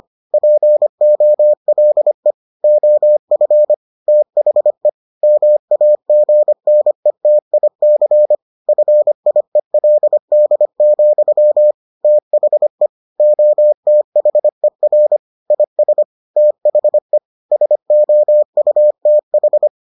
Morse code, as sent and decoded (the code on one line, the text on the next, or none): .--. --- .-.. / --- ..-. / - .... / -- .- --. -. - .. -.-. / ..-. .. .-.. -.. --..-- / - .... / --- - .... .-. / .. ... / - .... / ... --- ..- - ....
POLE OF THE MAGNETIC FIELD, THE OTHER IS THE SOUTH